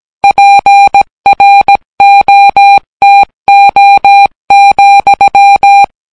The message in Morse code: PROTO,